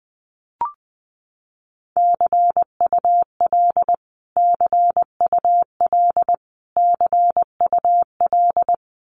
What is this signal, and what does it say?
Morse code, heard as a keyed tone, about 20 wpm, 700 Hz. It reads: CUL CUL CUL
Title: Abbreviations Repeated 3x - 20wpm